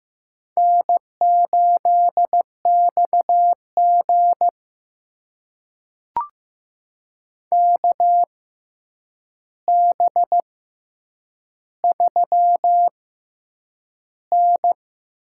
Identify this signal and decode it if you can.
Morse code, heard as a keyed tone, about 15 wpm, spaced as for 8 wpm, 700 Hz. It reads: N8XG KB3N
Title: USA Call Signs - 4 Characters - 6x Character Spacing - 15wpm